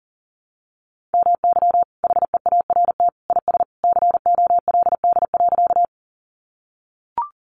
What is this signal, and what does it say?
Morse code, heard as a keyed tone, about 40 wpm, 700 Hz. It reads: MY HEART IS COLD.
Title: Sentences from Top 1200 Words 40wpm